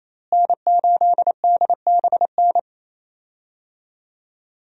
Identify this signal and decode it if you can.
Morse code, heard as a keyed tone, about 28 wpm, 700 Hz. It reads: N8DBN
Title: USA Call Signs - 5 Characters - Repeated 2x - 28wpm